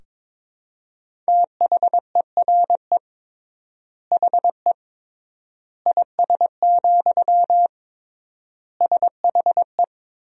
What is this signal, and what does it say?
Morse code, heard as a keyed tone, about 22 wpm, 700 Hz. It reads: THERE HE IS, SHE
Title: 3x Word-Spacing - Sentences from Top 100 Words - 22wpm